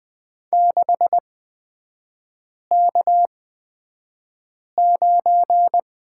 Morse code, heard as a keyed tone, about 20 wpm, spaced as for 5 wpm, 690 Hz. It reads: 6K9
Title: 229 Lesson 37e - Letter x - Three Characters 20wpm